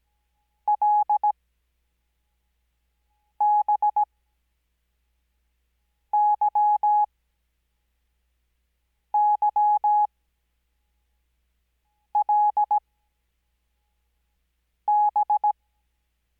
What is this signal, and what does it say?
Morse code, heard as a keyed tone, about 17 wpm, spaced as for 4 wpm, 840 Hz. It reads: LBYYLB
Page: Farnsworth Spacing: 15WPM Character speed Spaced at 5WPM.